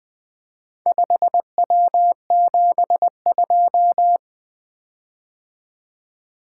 Morse code, heard as a keyed tone, about 20 wpm, 700 Hz. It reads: 5W72
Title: Four Letter-Number Combinations - 20wpm